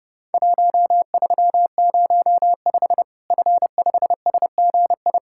Morse code, heard as a keyed tone, about 30 wpm, 700 Hz. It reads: 1305 F5SGI
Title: Worked All Europe DX Contest - QTC Exchange - 30wpm